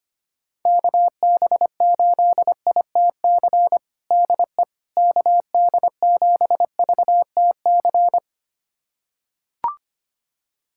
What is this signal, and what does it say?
Morse code, heard as a keyed tone, about 25 wpm, 700 Hz. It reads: KB8ITC DE KD7VTC
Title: USA Callsign DE Callsign - 6 Characters - 25wpm